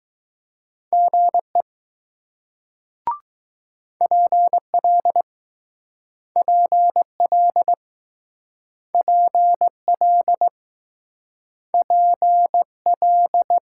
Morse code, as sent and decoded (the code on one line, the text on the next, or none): --. / .--. .-.. / .--. .-.. / .--. .-.. / .--. .-..
GE PL PL PL PL